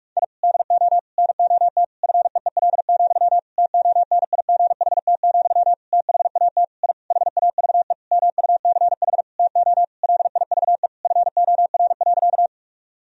Morse code, heard as a keyed tone, about 46 wpm, 700 Hz. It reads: I DO NOT FEEL, TONIGHT, THAT I HAVE MUCH TO LIVE FOR.
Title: Sentences from Top 1000 Words - Review Part 3 - 45wpm